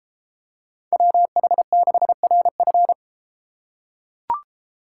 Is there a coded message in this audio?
Yes: Morse code from 0.9 to 2.9 s, heard as a keyed tone, about 32 wpm, 700 Hz. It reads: WH6RF